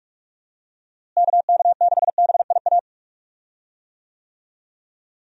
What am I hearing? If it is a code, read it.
Morse code, heard as a keyed tone, about 46 wpm, 700 Hz. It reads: KK6BIA
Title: USA Call Signs - 6 Characters - Repeated 3x - 45wpm